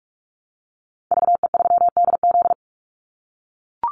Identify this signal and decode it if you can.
Morse code, heard as a keyed tone, about 44 wpm, 700 Hz. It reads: VE3DZ
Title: Call Signs - 45wpm